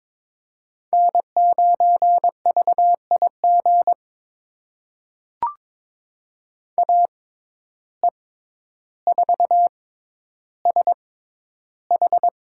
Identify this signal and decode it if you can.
Morse code, heard as a keyed tone, about 22 wpm, spaced as for 11 wpm, 700 Hz. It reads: N9VIG AE4SH
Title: USA Call Signs - 5 Characters - 6x Character Spacing - 22wpm